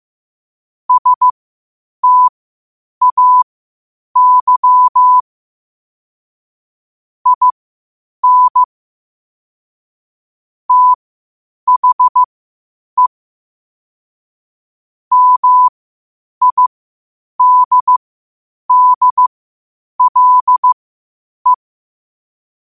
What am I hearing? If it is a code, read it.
Morse code, heard as a keyed tone, about 15 wpm, spaced as for 8 wpm, 1000 Hz. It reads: STAY IN THE MIDDLE
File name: qotdfiles / 8wpm